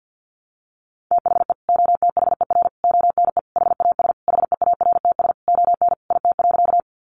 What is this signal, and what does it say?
Morse code, heard as a keyed tone, about 50 wpm, 700 Hz. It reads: THE OTHER ONE HAS HEARTS ON IT.
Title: Additional Easy Phrases 50wpm